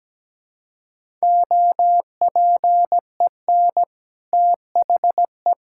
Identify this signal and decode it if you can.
Morse code, heard as a keyed tone, about 17 wpm, 700 Hz. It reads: OPEN THE
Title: Sentences from Top 1000 Words - Review Part 6 - 17wpm